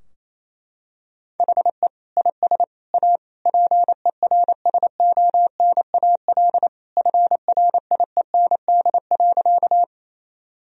Morse code, heard as a keyed tone, about 28 wpm, 700 Hz. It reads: HE IS A PERSONAL FRIEND.
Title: Sentences from Top 1000 Words - Review Part 4 - 28wpm